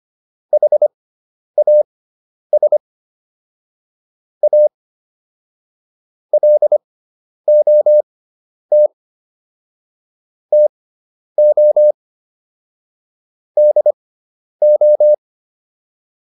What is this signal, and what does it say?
Morse code, heard as a keyed tone, about 25 wpm, spaced as for 10 wpm, 600 Hz. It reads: HAS A LOT TO DO